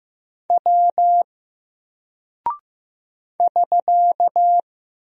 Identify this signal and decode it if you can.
Morse code, heard as a keyed tone, about 15 wpm, 700 Hz. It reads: W <SK>
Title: Single Character (Letter:Number:Punctuation:Prosign) 15wpm